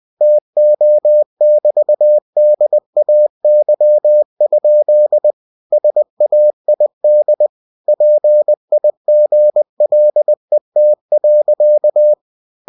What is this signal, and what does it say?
Morse code, heard as a keyed tone, about 20 wpm, 600 Hz. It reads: TO<BT>DAY? SAID PIGLET.
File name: cw-books / winnie-the-pooh / winnie-20wpm-page-0110